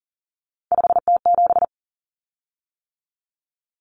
Morse code, heard as a keyed tone, about 40 wpm, 700 Hz. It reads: 5T7
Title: Common Cut Numbers - 3-Digit 40wpm